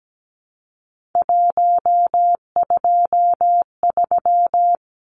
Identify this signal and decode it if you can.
Morse code, heard as a keyed tone, about 17 wpm, 700 Hz. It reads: 123
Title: Numbers - 3-Digit 17wpm